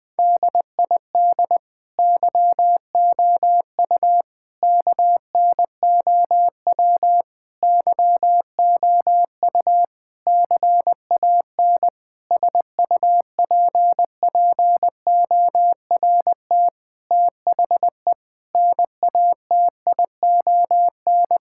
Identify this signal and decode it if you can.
Morse code, heard as a keyed tone, about 20 wpm, 700 Hz. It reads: DID YOU KNOW YOU CAN SUPPORT THE NATION
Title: Sets of 9 Words - 20wpm